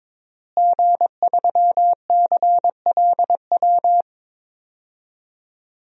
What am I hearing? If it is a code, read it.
Morse code, heard as a keyed tone, about 22 wpm, 700 Hz. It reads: G3CLW
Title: DXCC-100 Call Signs - 5 Characters - Repeated 3x - 22wpm